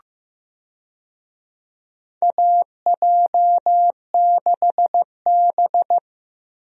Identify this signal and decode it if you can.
Morse code, heard as a keyed tone, about 15 wpm, 700 Hz. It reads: AJ6B
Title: USA Call Signs - 4 Characters - Repeated 3x - 15wpm